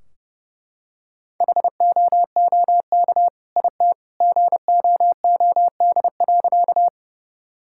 Morse code, heard as a keyed tone, about 32 wpm, 700 Hz. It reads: HOOK IT GOOD.
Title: Sentences from Top 1600 Words 30wpm